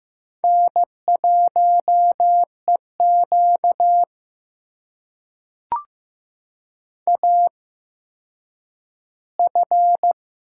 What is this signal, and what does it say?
Morse code, heard as a keyed tone, about 15 wpm, spaced as for 9 wpm, 700 Hz. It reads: N1EQ AF
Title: USA Call Signs - 4 Characters - 8x Character Spacing - 15wpm